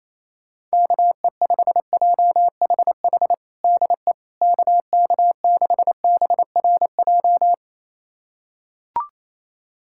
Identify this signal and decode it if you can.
Morse code, heard as a keyed tone, about 28 wpm, 700 Hz. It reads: KE5JHH DE KK6BRJ
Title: USA Callsign DE Callsign - 6 Characters - 28wpm